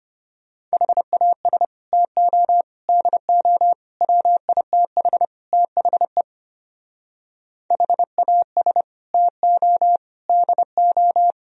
Morse code, heard as a keyed tone, about 33 wpm, 700 Hz. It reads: HAS TO DO WITH THE HAS TO DO